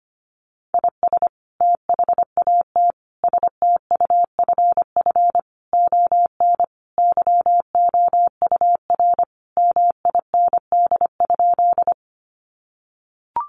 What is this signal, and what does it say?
Morse code, heard as a keyed tone, about 25 wpm, 700 Hz. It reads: IS THAT STUFF ON YOUR MIND?
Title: Sentences from Top 1400 Words 25wpm